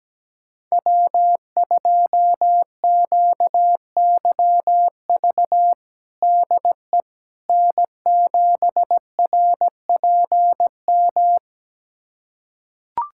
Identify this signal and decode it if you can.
Morse code, heard as a keyed tone, about 17 wpm, 700 Hz. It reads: W2QYV DE N7RPM
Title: USA Callsign DE Callsign - 5 Characters - 17wpm